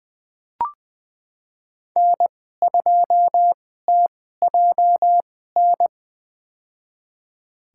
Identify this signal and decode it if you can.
Morse code, heard as a keyed tone, about 20 wpm, spaced as for 14 wpm, 700 Hz. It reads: N2TJN
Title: USA Call Signs - 5 Characters - 2x Character Spacing - 20wpm